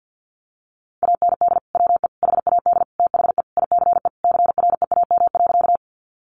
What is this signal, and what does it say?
Morse code, heard as a keyed tone, about 50 wpm, 700 Hz. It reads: AND WE HAD THE ICE CREAM.
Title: Sentences from Top 1500 Words 50wpm